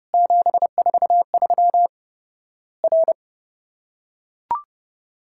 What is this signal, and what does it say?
Morse code, heard as a keyed tone, about 30 wpm, 700 Hz. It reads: 743 R
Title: Worked All Europe DX Contest - QTC Exchange - 30wpm